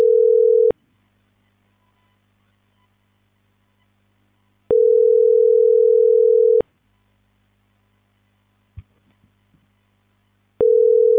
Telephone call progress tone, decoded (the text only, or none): ringback tone